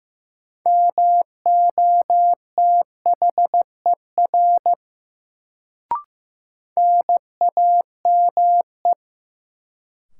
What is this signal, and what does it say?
Morse code, heard as a keyed tone, about 16 wpm, 700 Hz. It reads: MOTHER NAME
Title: Sentences from Top 200 Words - New Word Drill - 15wpm